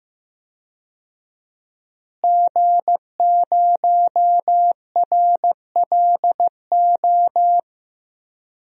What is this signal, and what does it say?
Morse code, heard as a keyed tone, about 15 wpm, 700 Hz. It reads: G0RLO
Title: DXCC-100 Call Signs - 5 Characters - Repeated 3x - 15wpm